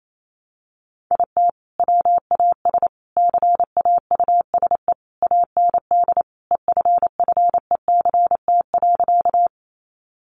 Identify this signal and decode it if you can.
Morse code, heard as a keyed tone, about 28 wpm, 700 Hz. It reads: IT WAS CAUSE AND EFFECT.
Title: Sentences from Top 1000 Words 28wpm